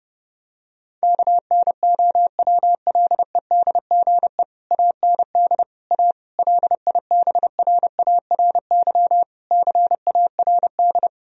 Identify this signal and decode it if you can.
Morse code, heard as a keyed tone, about 32 wpm, 700 Hz. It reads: KNOWLEDGE AND A LIBRARY CARD
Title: Sentences from Top 1000 Words - Review Part 4 - 30wpm